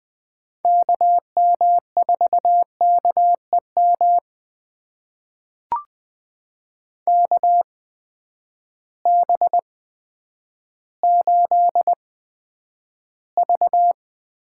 Morse code, heard as a keyed tone, about 20 wpm, spaced as for 10 wpm, 700 Hz. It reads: KM4KEM KB8V
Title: USA Call Signs - 6 Characters - 8x Character Spacing - 20wpm